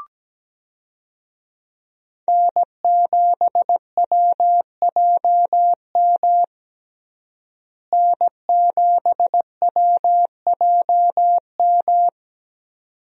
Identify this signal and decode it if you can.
Morse code, heard as a keyed tone, about 17 wpm, 700 Hz. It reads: N7WJM N7WJM
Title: USA Call Signs - 5 Characters - Repeated 2x - 17wpm